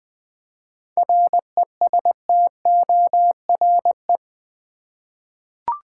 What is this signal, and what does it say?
Morse code, heard as a keyed tone, about 20 wpm, 700 Hz. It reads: RESTORE
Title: 7 Letter Words - First 3 Letters Given - 20wpm